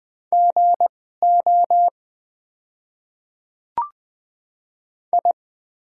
Morse code, heard as a keyed tone, about 21 wpm, spaced as for 15 wpm, 700 Hz. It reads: GO I
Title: 2x Character Spacing - Top 100 Words - 20wpm